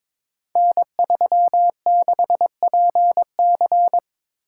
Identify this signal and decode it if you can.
Morse code, heard as a keyed tone, about 22 wpm, 700 Hz. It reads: N36PC
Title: Code Groups - 5 Letter:Numbers - 22wpm